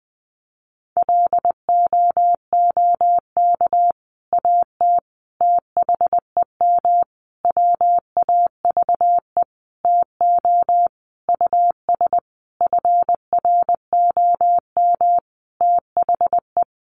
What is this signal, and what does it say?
Morse code, heard as a keyed tone, about 20 wpm, 700 Hz. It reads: LOOK AT THEM WAVE TO US FROM THE
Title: Sentences from Top 1400 Words 20wpm